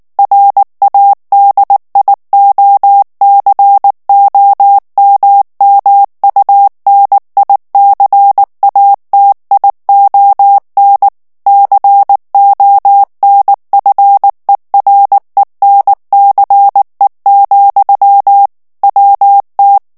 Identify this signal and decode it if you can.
Morse code, heard as a keyed tone, about 19 wpm, 800 Hz. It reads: RADIOCOMMUNICATION CONFERENCE, WT
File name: morse mp3 / ANGLAIS texte clair 3 vitesse 18 mots minute